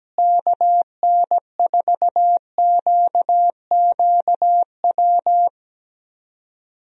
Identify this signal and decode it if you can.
Morse code, heard as a keyed tone, about 17 wpm, 700 Hz. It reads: KN4QQW
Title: USA Call Signs - 6 Characters - Repeated 2x - 17wpm